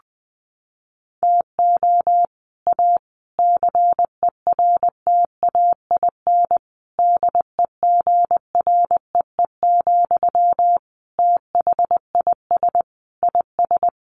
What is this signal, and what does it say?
Morse code, heard as a keyed tone, about 20 wpm, 700 Hz. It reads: TO A CERTAIN DEGREE, THIS IS
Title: Sentences from Top 1500 Words 20wpm